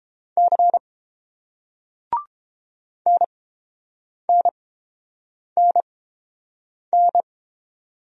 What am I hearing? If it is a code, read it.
Morse code, heard as a keyed tone, about 29 wpm, spaced as for 7 wpm, 690 Hz. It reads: C NNNN